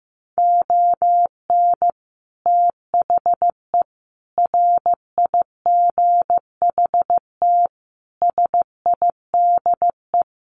Morse code, heard as a keyed tone, about 15 wpm, 700 Hz. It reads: ON THE RIGHT SIDE
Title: Sets of 5 Words Encore 15wpm